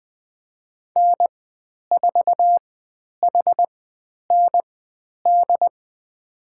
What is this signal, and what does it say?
Morse code, heard as a keyed tone, about 20 wpm, spaced as for 10 wpm, 700 Hz. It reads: N4HND
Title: 102 Lesson 17g - Letter u - Call Signs 20wpm